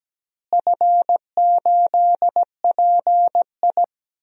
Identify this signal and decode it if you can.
Morse code, heard as a keyed tone, about 17 wpm, 700 Hz. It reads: F8PI
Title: DXCC-100 Call Signs - 4 Characters - Repeated 2x - 17wpm